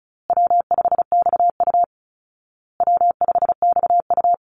Morse code, heard as a keyed tone, about 34 wpm, 700 Hz. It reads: W5XU W5XU
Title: Call Signs Repeated 2x - 35wpm